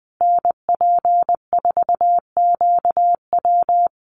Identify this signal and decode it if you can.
Morse code, heard as a keyed tone, about 20 wpm, 700 Hz. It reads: NP4QW
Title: USA Call Signs Repeated 3x 20wpm